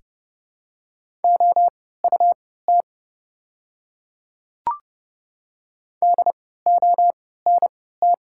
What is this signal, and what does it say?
Morse code, heard as a keyed tone, about 30 wpm, spaced as for 17 wpm, 690 Hz. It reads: OUT DONT
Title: 3x Character Spacing - Top 300 Words - 30wpm